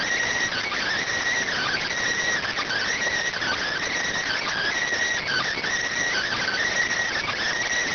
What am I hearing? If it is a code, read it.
Emergency Alert System (SAME) tones (header bursts), heard as harsh buzzing bursts. It is heard from 0.7 to 6.8 s.